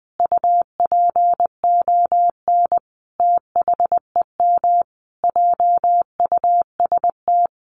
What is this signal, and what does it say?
Morse code, heard as a keyed tone, about 20 wpm, 700 Hz. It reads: UPON THEM JUST
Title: Sentences from Top 1100 Words 20wpm